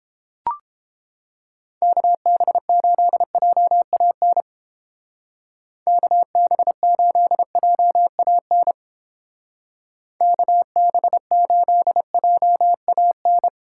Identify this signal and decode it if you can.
Morse code, heard as a keyed tone, about 30 wpm, 700 Hz. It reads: KB8JAN KB8JAN KB8JAN